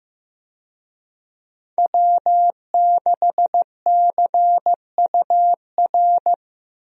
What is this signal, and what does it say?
Morse code, heard as a keyed tone, about 16 wpm, 700 Hz. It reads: W6CUR
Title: USA Call Signs - 5 Characters - Repeated 2x - 15wpm